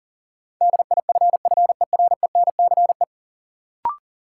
Morse code, heard as a keyed tone, about 37 wpm, 700 Hz. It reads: DIFFERENCE
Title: Sentences from Top 800 Words - New Word Drill - 40wpm